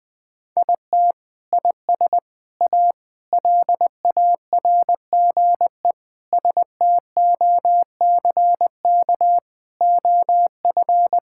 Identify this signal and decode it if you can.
Morse code, heard as a keyed tone, about 19 wpm, 700 Hz. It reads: IT IS A LARGE STOCK OF
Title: Sentences from Top 1000 Words - Review Part 5 - 20wpm